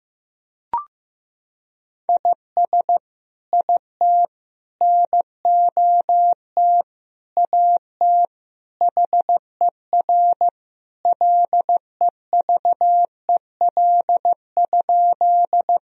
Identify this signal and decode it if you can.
Morse code, heard as a keyed tone, about 16 wpm, 700 Hz. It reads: IS IT NOT AT HER LEVEL?
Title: Sentences from Top 1000 Words - Review Part 4 - 15wpm